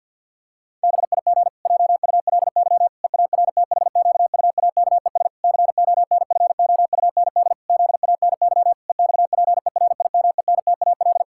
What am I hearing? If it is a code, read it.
Morse code, heard as a keyed tone, about 51 wpm, 700 Hz. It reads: BIG JULY EARTHQUAKES CONFOUND ZANY EXPERIMENTAL
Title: Pangram Challenge v2 - 50wpm